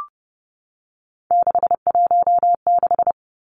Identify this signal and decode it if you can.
Morse code, heard as a keyed tone, about 30 wpm, 700 Hz. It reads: 616
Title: Three Challenging Letter-Number Combinations 30wpm